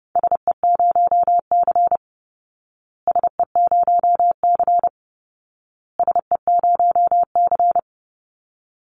Morse code, heard as a keyed tone, about 30 wpm, 700 Hz. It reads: SE0C SE0C SE0C
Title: Call Signs Repeated 2x - 30wpm